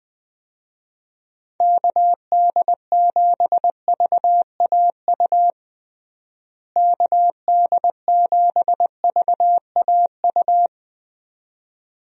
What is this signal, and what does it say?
Morse code, heard as a keyed tone, about 21 wpm, 700 Hz. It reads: KD7VAU KD7VAU
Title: USA Call Signs - 6 Characters - Repeated 2x - 20wpm